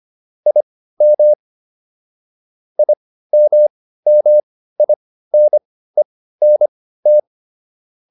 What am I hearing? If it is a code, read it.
Morse code, heard as a keyed tone, about 25 wpm, spaced as for 15 wpm, 600 Hz. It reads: IM IMMINENT